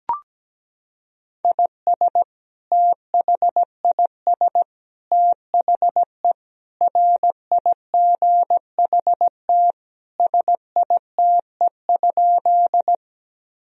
Morse code, heard as a keyed tone, about 17 wpm, 700 Hz. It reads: IS THIS THE RIGHT SITE?
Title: Sentences from Top 1000 Words - Review Part 4 - 17wpm